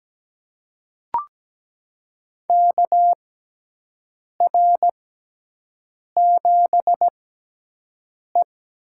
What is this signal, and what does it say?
Morse code, heard as a keyed tone, about 17 wpm, spaced as for 6 wpm, 700 Hz. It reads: KR7E
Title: USA Call Signs - 4 Characters - 6x Character Spacing - 17wpm